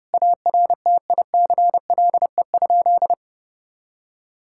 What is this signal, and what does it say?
Morse code, heard as a keyed tone, about 30 wpm, 700 Hz. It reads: ARTICLE?
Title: Sentences from Top 1000 Words - Review Part 5 - 30wpm